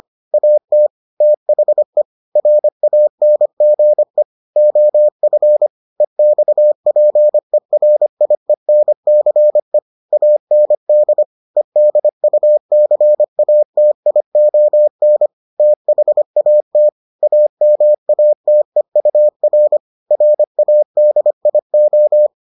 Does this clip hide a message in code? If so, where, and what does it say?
Yes: Morse code from 0.3 to 22.3 s, heard as a keyed tone, about 25 wpm, 600 Hz. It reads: AT THE RANGE OF EXPERIENCE AND EDUCATION THAT AMATEUR RADIO